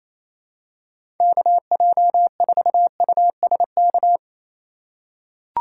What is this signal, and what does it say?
Morse code, heard as a keyed tone, about 28 wpm, 700 Hz. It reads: KJ4USK
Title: USA Call Signs - 6 Characters - 3x Character Spacing - 28wpm